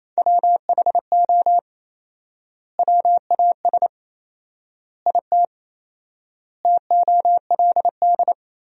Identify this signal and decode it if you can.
Morse code, heard as a keyed tone, about 28 wpm, 700 Hz. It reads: WHO WAS IT TOLD
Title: 4x Word-Spacing - Sentences from Top 400 Words - 28wpm